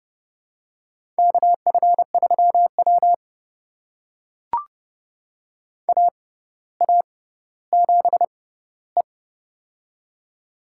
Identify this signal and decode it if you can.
Morse code, heard as a keyed tone, about 32 wpm, spaced as for 16 wpm, 700 Hz. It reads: KF3W AA7E
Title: USA Call Signs - 4 Characters - 6x Character Spacing - 30wpm